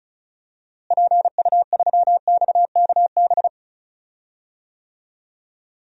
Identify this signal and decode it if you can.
Morse code, heard as a keyed tone, about 34 wpm, 700 Hz. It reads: PU3XKB
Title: DXCC-100 Call Signs - 6 Characters - Repeated 2x - 35wpm